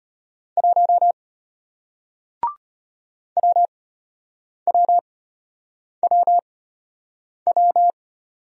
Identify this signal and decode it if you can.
Morse code, heard as a keyed tone, about 33 wpm, spaced as for 8 wpm, 700 Hz. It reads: 1 WWWW